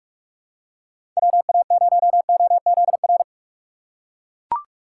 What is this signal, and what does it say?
Morse code, heard as a keyed tone, about 41 wpm, 700 Hz. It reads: WA0OZR
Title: USA Call Signs - 6 Characters - 3x Character Spacing - 45wpm